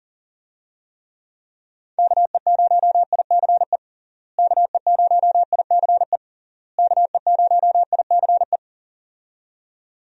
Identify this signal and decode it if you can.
Morse code, heard as a keyed tone, about 37 wpm, 700 Hz. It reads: KE0ICE KE0ICE KE0ICE
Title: USA Call Signs - 6 Characters - Repeated 3x - 40wpm